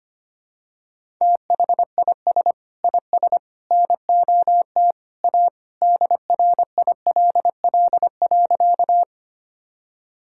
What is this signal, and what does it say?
Morse code, heard as a keyed tone, about 25 wpm, 700 Hz. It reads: THIS IS NOT A DRILL.
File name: practice / 25wpm / Sentences with 3-Word Phrases - 25wpm